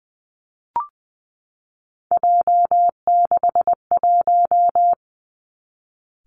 Morse code, heard as a keyed tone, about 20 wpm, 700 Hz. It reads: J61
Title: Three Challenging Letter-Number Combinations 20wpm